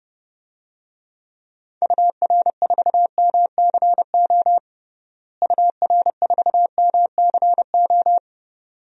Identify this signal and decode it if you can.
Morse code, heard as a keyed tone, about 29 wpm, 700 Hz. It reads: UR4MCO UR4MCO
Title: DXCC-100 Call Signs - 6 Characters - Repeated 2x - 30wpm